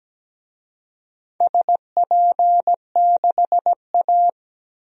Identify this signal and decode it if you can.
Morse code, heard as a keyed tone, about 17 wpm, 700 Hz. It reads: SP6A
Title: DXCC-100 Call Signs - 4 Characters - Repeated 2x - 17wpm